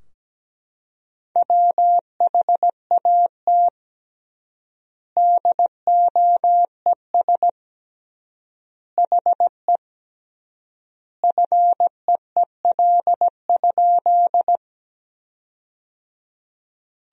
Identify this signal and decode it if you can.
Morse code, heard as a keyed tone, about 17 wpm, 700 Hz. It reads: WHAT DOES HE FEEL?
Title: 3x Word-Spacing - Sentences from Top 400 Words - 17wpm